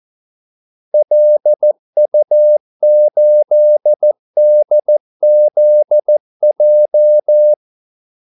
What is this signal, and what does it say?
Morse code, heard as a keyed tone, about 14 wpm, 600 Hz. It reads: LU8DZJ
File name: Callsigns-14wpm